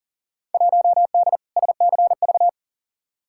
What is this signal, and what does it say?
Morse code, heard as a keyed tone, about 40 wpm, 700 Hz. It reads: 1D SCV
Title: Field Day Exchanges Encore - 40wpm